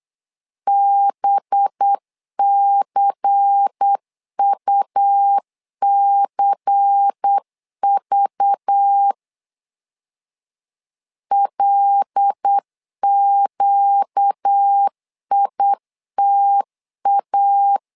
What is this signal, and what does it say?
Morse code, heard as a keyed tone, about 8 wpm, 800 Hz. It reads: BCUCV LQITA
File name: LEZIONI / 5 LBVC / 5 LBVC 8wpm 2935